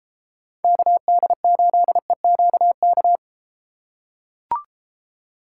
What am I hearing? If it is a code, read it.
Morse code, heard as a keyed tone, about 34 wpm, 700 Hz. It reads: KD8EQK